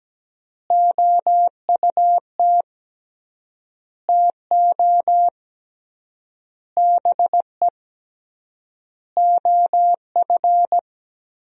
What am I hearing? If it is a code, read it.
Morse code, heard as a keyed tone, about 17 wpm, 700 Hz. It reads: OUT TO BE OF
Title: 3x Word-Spacing - Sentences from Top 400 Words - 17wpm